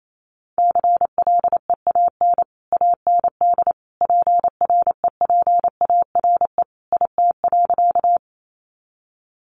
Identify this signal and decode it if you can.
Morse code, heard as a keyed tone, about 28 wpm, 700 Hz. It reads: CLEAN AND PREPARE IT.
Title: Sentences from Top 1100 Words 28wpm